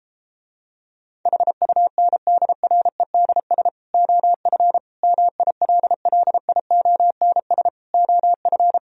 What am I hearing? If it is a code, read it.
Morse code, heard as a keyed tone, about 33 wpm, 700 Hz. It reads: HUNDREDS OF MILLIONS OF